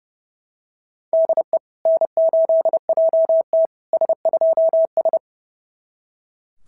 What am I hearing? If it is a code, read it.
Morse code, heard as a keyed tone, about 30 wpm, 650 Hz. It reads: DE N8JT S2S
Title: SOTA QSO Practice 30wpm